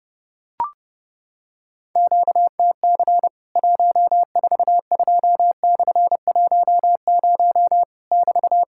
Morse code, heard as a keyed tone, about 31 wpm, 700 Hz. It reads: QTC 142/10 <BT>
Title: Worked All Europe DX Contest - QTC Exchange - 30wpm